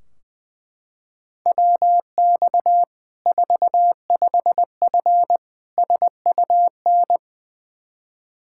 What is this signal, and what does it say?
Morse code, heard as a keyed tone, about 20 wpm, 700 Hz. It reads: WX 45F SUN
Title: K1USN SST QSO Practice 20wpm